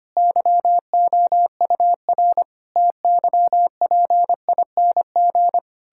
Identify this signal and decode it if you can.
Morse code, heard as a keyed tone, about 25 wpm, 700 Hz. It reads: YOUR TYPING
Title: Pangram Challenge v2 - 25wpm